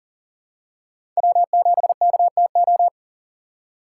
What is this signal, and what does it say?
Morse code, heard as a keyed tone, about 38 wpm, 700 Hz. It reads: W7KTO
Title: USA Call Signs - 6x Character Spacing - 40wpm